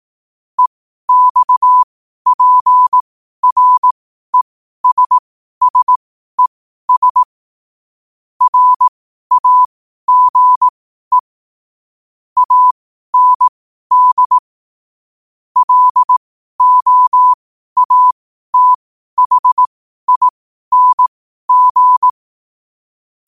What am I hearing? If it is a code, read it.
Morse code, heard as a keyed tone, about 18 wpm, spaced as for 12 wpm, 1000 Hz. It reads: EXPRESSES RAGE AND LOATHING
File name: qotdfiles / 12wpm